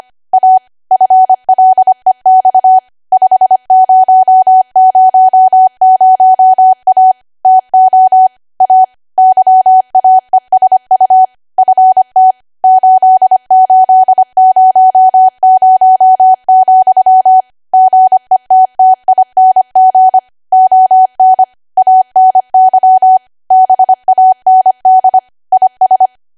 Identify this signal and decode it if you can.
Morse code, heard as a keyed tone, about 25 wpm, 750 Hz. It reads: A FLEX 5000A TO A YAESU FT 8800, GETTING ON ANY BAND IS
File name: Archive / 25 WPM / 25WPM